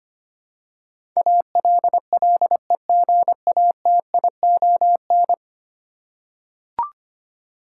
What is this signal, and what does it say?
Morse code, heard as a keyed tone, about 25 wpm, 700 Hz. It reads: ALLEGATION
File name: 10 Letter Words - First 3 Letters Given - 25wpm